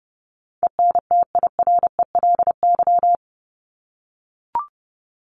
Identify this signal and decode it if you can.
Morse code, heard as a keyed tone, about 30 wpm, 700 Hz. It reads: ENTIRELY
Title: LY Suffix - Top 100 Words with Base Words 30wpm